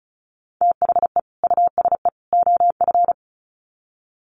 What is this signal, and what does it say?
Morse code, heard as a keyed tone, about 34 wpm, 700 Hz. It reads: THE USE OF
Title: Sets of 3 Words 35wpm